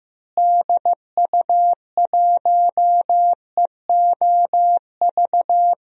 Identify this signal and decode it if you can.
Morse code, heard as a keyed tone, about 16 wpm, 700 Hz. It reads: DU1EOV
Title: DXCC-100 Call Signs - 6 Characters - Repeated 2x - 15wpm